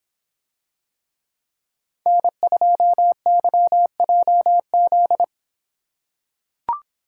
Morse code, heard as a keyed tone, about 26 wpm, 700 Hz. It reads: N2YJZ